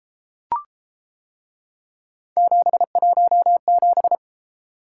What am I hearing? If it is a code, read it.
Morse code, heard as a keyed tone, about 32 wpm, 700 Hz. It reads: 717